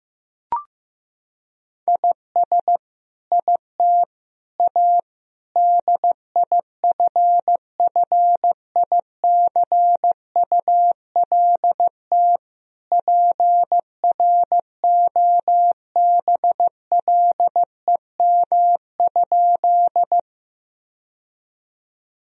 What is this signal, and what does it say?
Morse code, heard as a keyed tone, about 15 wpm, 700 Hz. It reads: IS IT A DIFFICULT PROBLEM?
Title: Sentences from Top 1000 Words - Review Part 2 - 15wpm